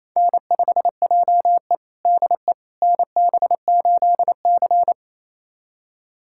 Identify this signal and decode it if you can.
Morse code, heard as a keyed tone, about 28 wpm, 700 Hz. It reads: N5JE DE NB8C
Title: USA Callsign DE Callsign - 4 Characters - 28wpm